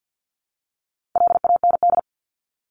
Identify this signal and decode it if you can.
Morse code, heard as a keyed tone, about 48 wpm, 700 Hz. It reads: LAND
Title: Top 300 Words 50wpm